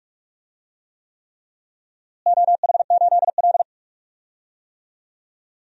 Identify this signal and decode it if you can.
Morse code, heard as a keyed tone, about 44 wpm, 700 Hz. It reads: OH8L
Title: DXCC-100 Call Signs - 4 Characters - Repeated 3x - 45wpm